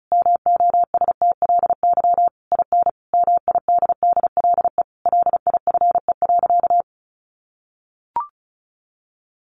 Morse code, heard as a keyed tone, about 34 wpm, 700 Hz. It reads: MOSTLY IN MIDDLE LIFE.
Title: Sentences from Top 1300 Words 35wpm